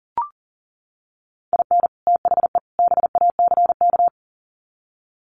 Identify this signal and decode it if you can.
Morse code, heard as a keyed tone, about 40 wpm, 700 Hz. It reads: IN THE BACK
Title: Sets of 3 Words 40wpm